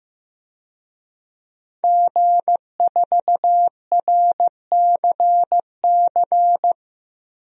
Morse code, heard as a keyed tone, about 14 wpm, 700 Hz. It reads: G4RCC
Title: DXCC-100 Call Signs - 5 Characters - Repeated 2x - 15wpm